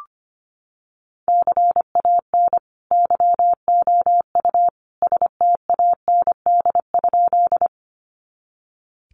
Morse code, heard as a keyed tone, about 25 wpm, 700 Hz. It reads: CAN YOU STAND?
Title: Sentences from Top 300 Words 25wpm